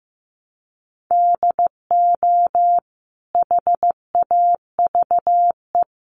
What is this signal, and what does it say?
Morse code, heard as a keyed tone, about 15 wpm, 700 Hz. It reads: DO HAVE
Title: Sentences from Top 1300 Words 15wpm